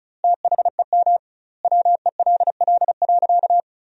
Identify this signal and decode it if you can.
Morse code, heard as a keyed tone, about 36 wpm, 700 Hz. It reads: THEM WELL.
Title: 2x Word-Spacing - Sentences from Top 300 Words - 35wpm